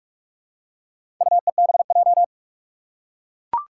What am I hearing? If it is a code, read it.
Morse code, heard as a keyed tone, about 44 wpm, 700 Hz. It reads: UEBJ
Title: Four Challenging Letter-Number Combinations 45wpm